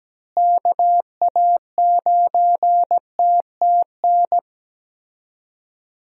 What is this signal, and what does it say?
Morse code, heard as a keyed tone, about 17 wpm, 700 Hz. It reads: KA9TTN
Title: USA Call Signs - 6 Characters - Repeated 2x - 17wpm